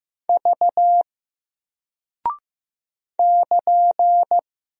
Morse code, heard as a keyed tone, about 15 wpm, 700 Hz. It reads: V <KN>
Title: Single Character (Letter:Number:Punctuation:Prosign) 15wpm